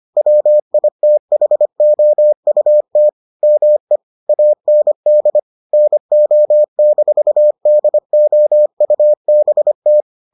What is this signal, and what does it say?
Morse code, heard as a keyed tone, about 25 wpm, 600 Hz. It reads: WITHOUT ME AND NO-DOUBT